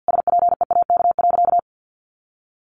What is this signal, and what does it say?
Morse code, heard as a keyed tone, about 50 wpm, 700 Hz. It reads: SPEAK.
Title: Sentences from Top 1500 Words 50wpm